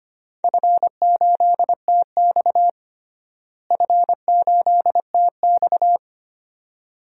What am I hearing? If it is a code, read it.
Morse code, heard as a keyed tone, about 25 wpm, 700 Hz. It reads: F8TX F8TX
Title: DXCC-100 Call Signs - 4 Characters - Repeated 2x - 25wpm